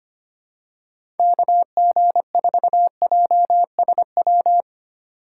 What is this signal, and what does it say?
Morse code, heard as a keyed tone, about 25 wpm, 700 Hz. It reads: KG4JSW
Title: USA Call Signs - 4x Character Spacing - 25wpm